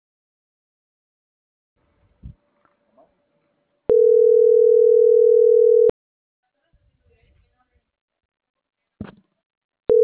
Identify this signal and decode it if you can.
Ringback tone (North America), 440 + 480 Hz.